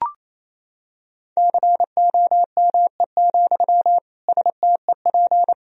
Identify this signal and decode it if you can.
Morse code, heard as a keyed tone, about 28 wpm, 700 Hz. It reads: COME, STEP
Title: Sentences from Top 1000 Words - Review Part 2 - 28wpm